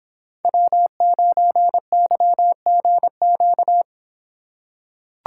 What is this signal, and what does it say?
Morse code, heard as a keyed tone, about 26 wpm, 700 Hz. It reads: W9YGQ